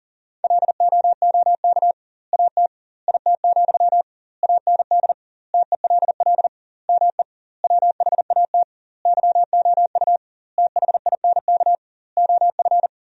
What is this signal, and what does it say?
Morse code, heard as a keyed tone, about 40 wpm, 700 Hz. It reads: LOOK AT IT, AND TELL ME WHAT YOU THINK OF
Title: 2x Word-Spacing - Sentences from Top 200 Words - 40wpm